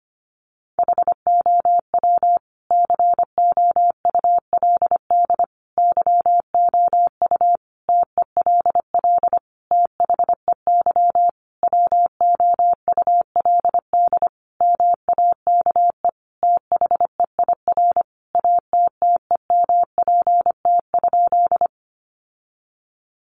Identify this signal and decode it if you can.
Morse code, heard as a keyed tone, about 25 wpm, 700 Hz. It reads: HOW COULD YOU TELL THEY WOULD MAKE THEIR ATTEMPT?
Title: Sentences from Top 1000 Words 25wpm